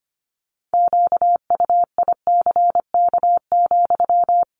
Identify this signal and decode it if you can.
Morse code, heard as a keyed tone, about 25 wpm, 700 Hz. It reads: QUICK,
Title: Sentences from Top 500 Words 25wpm